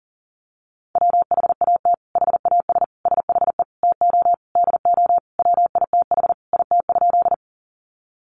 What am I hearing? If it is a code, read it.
Morse code, heard as a keyed tone, about 40 wpm, 700 Hz. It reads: WHAT HAS SHE TO DO WITH IT?
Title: Sentences from Top 100 Words 40wpm